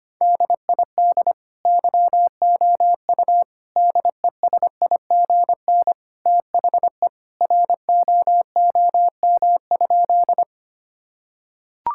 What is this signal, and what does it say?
Morse code, heard as a keyed tone, about 25 wpm, 700 Hz. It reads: DID YOU DESIGN THE ROOM?
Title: Sentences from Top 1000 Words - Review Part 5 - 25wpm